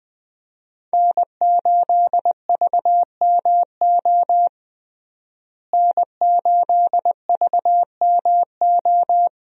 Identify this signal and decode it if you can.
Morse code, heard as a keyed tone, about 20 wpm, 700 Hz. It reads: N8VMO N8VMO
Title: USA Call Signs - 5 Characters - Repeated 3x - 20wpm